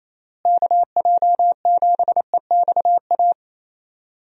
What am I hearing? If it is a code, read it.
Morse code, heard as a keyed tone, about 28 wpm, 700 Hz. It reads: KJ7EXA
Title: USA Call Signs - 6 Characters - Repeated 2x - 28wpm